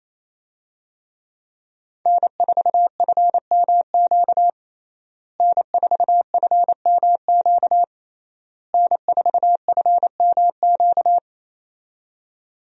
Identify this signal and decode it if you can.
Morse code, heard as a keyed tone, about 27 wpm, 700 Hz. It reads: N4FMQ N4FMQ N4FMQ
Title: USA Call Signs - 5 Characters - Repeated 3x - 28wpm